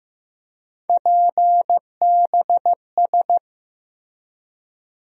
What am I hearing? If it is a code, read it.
Morse code, heard as a keyed tone, about 15 wpm, 700 Hz. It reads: PBS
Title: USA Call Sign Postfixes - 3 Letters 15wpm